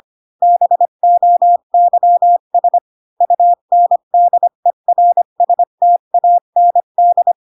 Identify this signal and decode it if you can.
Morse code, heard as a keyed tone, about 25 wpm, 700 Hz. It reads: BOYS UNDERSTAND
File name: Pinocchio / Pinocchio ch8 20 25 ik1hge